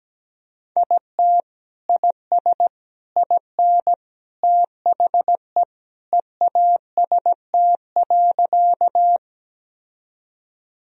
Morse code, heard as a keyed tone, about 17 wpm, 700 Hz. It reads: IT IS IN THE EAST.
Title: Sentences from Top 1000 Words - Review Part 3 - 17wpm